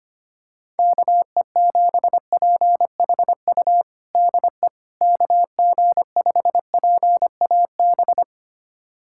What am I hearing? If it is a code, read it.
Morse code, heard as a keyed tone, about 25 wpm, 700 Hz. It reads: KE7PHU DE KG5PAB
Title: USA Callsign DE Callsign - 6 Characters - 25wpm